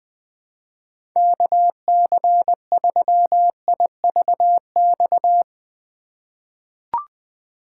Morse code, heard as a keyed tone, about 20 wpm, 700 Hz. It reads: KC3IVX
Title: USA Call Signs - 6 Characters - 3x Character Spacing - 20wpm